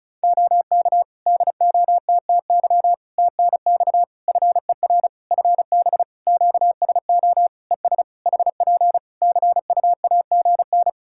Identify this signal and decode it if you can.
Morse code, heard as a keyed tone, about 36 wpm, 700 Hz. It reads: OK DOTTY TNX FER FB QSO ES HP CUAGN
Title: Practice QSO - Exchange 4 - Ending - 35wpm